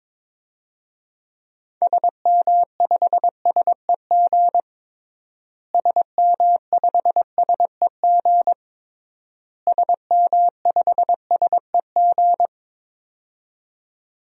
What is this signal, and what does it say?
Morse code, heard as a keyed tone, about 22 wpm, 700 Hz. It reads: SM5SEG SM5SEG SM5SEG
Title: DXCC-100 Call Signs - 6 Characters - Repeated 3x - 22wpm